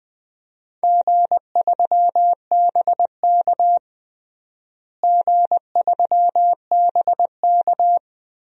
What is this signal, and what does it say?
Morse code, heard as a keyed tone, about 19 wpm, 690 Hz. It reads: G3BK G3BK
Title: DXCC-100 Call Signs - 4 Characters - Repeated 3x - 20wpm